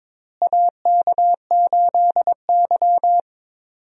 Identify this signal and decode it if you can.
Morse code, heard as a keyed tone, about 22 wpm, 700 Hz. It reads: AK8Y
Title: USA Call Signs - 4 Characters - Repeated 2x - 22wpm